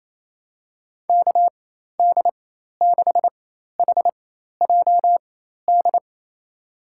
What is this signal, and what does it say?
Morse code, heard as a keyed tone, about 27 wpm, spaced as for 13 wpm, 700 Hz. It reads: KD6HJD
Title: USA Call Signs - 6 Characters - 4x Character Spacing - 28wpm